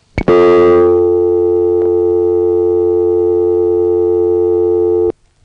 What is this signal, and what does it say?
Dial tone (North America), 350 + 440 Hz.
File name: phonetrips / HarrisonbgVA1HQ